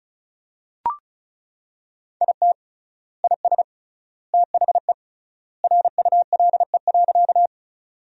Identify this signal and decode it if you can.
Morse code, heard as a keyed tone, about 34 wpm, 700 Hz. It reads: IT IS THE RULE.
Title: 3x Word-Spacing - Sentences from Top 400 Words - 35wpm